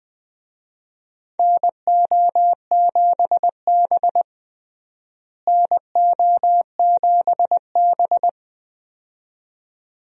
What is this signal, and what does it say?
Morse code, heard as a keyed tone, about 20 wpm, 700 Hz. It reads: NO7B NO7B
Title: USA Call Signs - 4 Characters - Repeated 2x - 20wpm